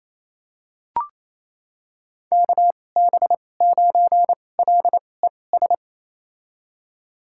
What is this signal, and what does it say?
Morse code, heard as a keyed tone, about 28 wpm, spaced as for 20 wpm, 700 Hz. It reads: KB9LES
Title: USA Call Signs - 6 Characters - 2x Character Spacing - 28wpm